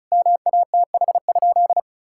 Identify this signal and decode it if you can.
Morse code, heard as a keyed tone, about 36 wpm, 700 Hz. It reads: MATH?
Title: Sentences from Top 1000 Words - Review Part 6 - 35wpm